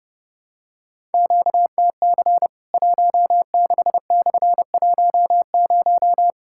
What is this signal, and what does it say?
Morse code, heard as a keyed tone, about 32 wpm, 700 Hz. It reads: QTC 16/10
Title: Worked All Europe DX Contest - QTC Exchange - 30wpm